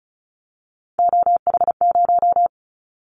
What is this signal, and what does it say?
Morse code, heard as a keyed tone, about 36 wpm, 700 Hz. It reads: OH0
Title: Callsign Prefixes for Top 100 DXCC - 35wpm